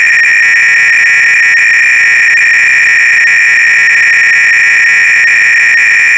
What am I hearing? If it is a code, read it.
300-baud modem data.